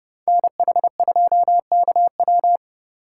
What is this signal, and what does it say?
Morse code, heard as a keyed tone, about 30 wpm, 700 Hz. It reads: NH2KW
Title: USA Call Signs - 5 Characters - 4x Character Spacing - 30wpm